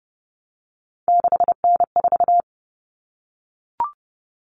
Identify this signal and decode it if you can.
Morse code, heard as a keyed tone, about 30 wpm, 700 Hz. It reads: 6N4
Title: Common Cut Numbers - 3-Digit 30wpm